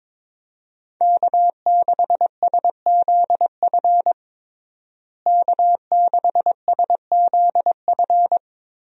Morse code, heard as a keyed tone, about 22 wpm, 700 Hz. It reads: K6SZF K6SZF
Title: USA Call Signs - 5 Characters - Repeated 3x - 22wpm